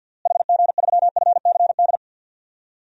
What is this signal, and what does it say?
Morse code, heard as a keyed tone, about 53 wpm, 700 Hz. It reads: HG3FCD
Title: DXCC-100 Call Signs - 6 Characters - Repeated 2x - 50wpm